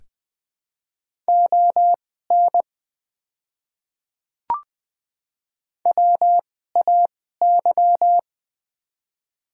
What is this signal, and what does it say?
Morse code, heard as a keyed tone, about 20 wpm, spaced as for 15 wpm, 700 Hz. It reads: ON WAY
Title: 2x Character Spacing - Top 100 Words - 20wpm